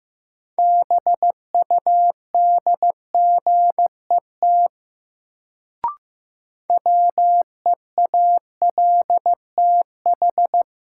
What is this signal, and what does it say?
Morse code, heard as a keyed tone, about 15 wpm, 700 Hz. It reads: BUDGET WEALTH
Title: Sentences from Top 1200 Words - New Word Drill Alt - 15wpm